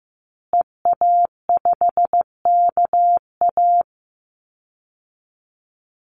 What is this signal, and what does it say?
Morse code, heard as a keyed tone, about 15 wpm, 700 Hz. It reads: EA5KA
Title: Call Signs Repeated 2x - 15wpm